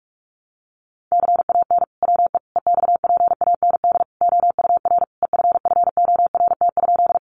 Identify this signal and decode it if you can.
Morse code, heard as a keyed tone, about 44 wpm, 700 Hz. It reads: CAN WE EXPAND OUR EFFORT?
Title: Sentences from Top 1400 Words 45wpm